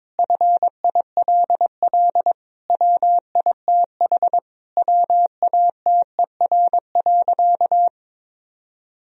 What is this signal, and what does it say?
Morse code, heard as a keyed tone, about 22 wpm, 700 Hz. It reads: FILL WITH WATER.
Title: Sentences from Top 1000 Words - Review Part 3 - 22wpm